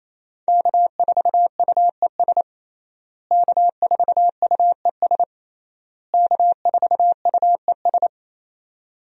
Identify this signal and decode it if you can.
Morse code, heard as a keyed tone, about 27 wpm, 700 Hz. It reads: K4UES K4UES K4UES
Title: USA Call Signs - 5 Characters - Repeated 3x - 28wpm